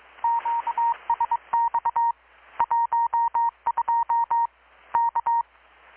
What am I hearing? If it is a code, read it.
Morse code, heard as a keyed tone, about 24 wpm, 930 Hz. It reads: QSX 12 K
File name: blog / swl / 6-abril-2019-12695-USB-KPS